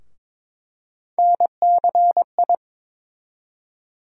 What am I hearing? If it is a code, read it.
Morse code, heard as a keyed tone, about 21 wpm, 700 Hz. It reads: NCI
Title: USA Call Sign Postfixes - 3 Letters 22wpm